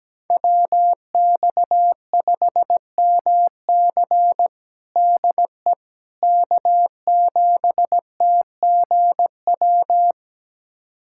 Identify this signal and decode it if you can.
Morse code, heard as a keyed tone, about 17 wpm, 700 Hz. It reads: WX5MC DE K7TGW
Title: USA Callsign DE Callsign - 5 Characters - 17wpm